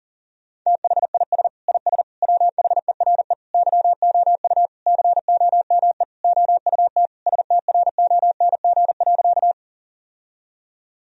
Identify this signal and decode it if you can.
Morse code, heard as a keyed tone, about 37 wpm, 700 Hz. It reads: THIS IS WHERE YOU COME OUT STRONG.
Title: Sentences from Top 1000 Words - Review Part 3 - 40wpm